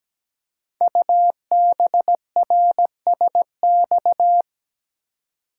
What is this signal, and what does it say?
Morse code, heard as a keyed tone, about 17 wpm, 700 Hz. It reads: UBRSX
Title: Code Groups - 5 Letters - 17wpm